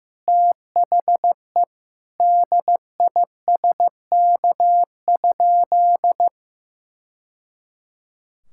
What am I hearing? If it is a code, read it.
Morse code, heard as a keyed tone, about 15 wpm, 700 Hz. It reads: THE DISK?
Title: Sentences from Top 1000 Words - Review Part 6 - 15wpm